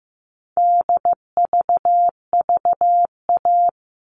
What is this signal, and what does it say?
Morse code, heard as a keyed tone, about 15 wpm, 700 Hz. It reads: DVVA
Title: All Cut Numbers - 4-Digit 15wpm